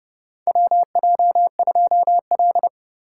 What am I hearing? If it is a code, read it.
Morse code, heard as a keyed tone, about 32 wpm, 700 Hz. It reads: WJ2L
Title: K1USN SST QSO Practice - 30wpm